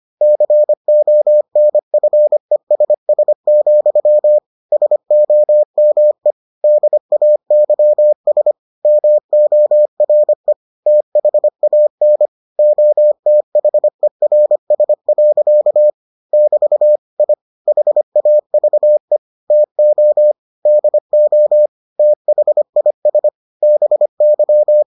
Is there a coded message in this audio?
Yes: Morse code from 0.2 to 24.8 s, heard as a keyed tone, about 25 wpm, 600 Hz. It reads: CONFESS, SOME DAYS MORE THAN OTHERS. <BT> I HAVE TO DO THIS BY